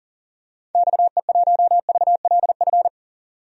Morse code, heard as a keyed tone, about 40 wpm, 700 Hz. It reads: XE1VLF
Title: DXCC-100 Call Signs - 6 Characters - 40wpm